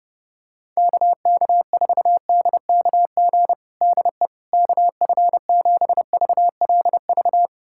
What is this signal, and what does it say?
Morse code, heard as a keyed tone, about 32 wpm, 700 Hz. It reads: KK4DKG DE KF7VLV
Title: USA Callsign DE Callsign 30wpm